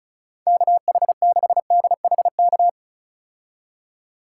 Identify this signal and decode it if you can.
Morse code, heard as a keyed tone, about 36 wpm, 700 Hz. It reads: KH6DHK
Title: K1USN SST QSO Practice - 35wpm